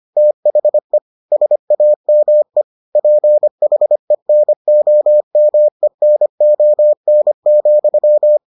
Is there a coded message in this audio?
Yes: Morse code from 0.2 to 8.4 s, heard as a keyed tone, about 25 wpm, 600 Hz. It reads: THE SAME PHENOMENON,